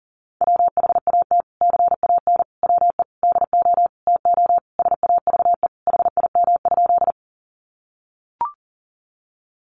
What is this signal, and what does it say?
Morse code, heard as a keyed tone, about 40 wpm, 700 Hz. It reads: WHAT CAN WE DO TO SAVE HIM?
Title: Sentences from Top 1100 Words 40wpm